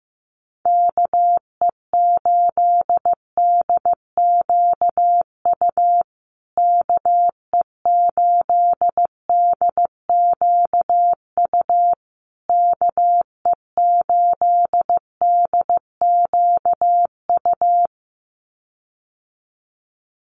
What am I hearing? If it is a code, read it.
Morse code, heard as a keyed tone, about 15 wpm, 700 Hz. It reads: KE8DQU KE8DQU KE8DQU
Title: USA Call Signs Repeated 3x 15wpm